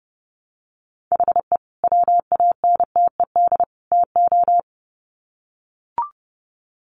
Morse code, heard as a keyed tone, about 28 wpm, 700 Hz. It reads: HE WANTED TO
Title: Sets of 3 Words 30wpm